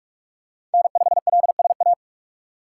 Morse code, heard as a keyed tone, about 41 wpm, 700 Hz. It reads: N5LSA
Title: USA Call Signs - 5 Characters - 2x Character Spacing - 45wpm